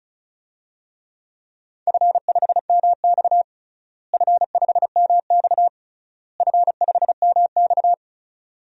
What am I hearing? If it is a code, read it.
Morse code, heard as a keyed tone, about 34 wpm, 700 Hz. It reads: F5MX F5MX F5MX
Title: DXCC-100 Call Signs - 4 Characters - Repeated 3x - 35wpm